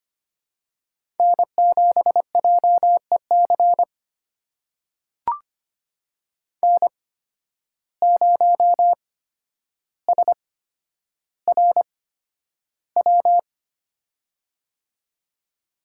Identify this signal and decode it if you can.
Morse code, heard as a keyed tone, about 25 wpm, spaced as for 11 wpm, 700 Hz. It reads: N7JEC N0SRW
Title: USA Call Signs - 5 Characters - 8x Character Spacing - 25wpm